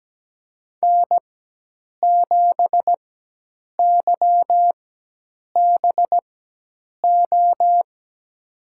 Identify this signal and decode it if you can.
Morse code, heard as a keyed tone, about 17 wpm, spaced as for 8 wpm, 700 Hz. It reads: N7YBO
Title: USA Call Signs - 5 Characters - 4x Character Spacing - 17wpm